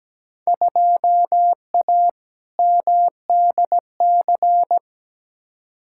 Morse code, heard as a keyed tone, about 17 wpm, 700 Hz. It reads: 2A MDC
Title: Field Day Exchanges Encore - 17wpm